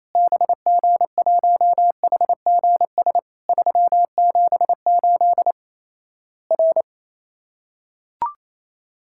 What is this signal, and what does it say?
Morse code, heard as a keyed tone, about 28 wpm, 700 Hz. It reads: BG1HGS 378 R
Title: Worked All Europe DX Contest - QTC Exchange - 28wpm